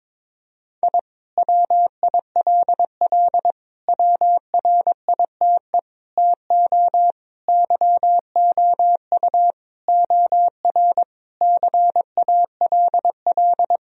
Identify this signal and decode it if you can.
Morse code, heard as a keyed tone, about 22 wpm, 700 Hz. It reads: I WILL WRITE TO YOU OR CALL
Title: Sentences from Top 1000 Words - Review Part 1 - 22wpm